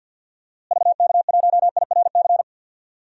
Morse code, heard as a keyed tone, about 50 wpm, 700 Hz. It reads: VK1IRC
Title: DXCC-100 Call Signs - 6 Characters - Repeated 2x - 50wpm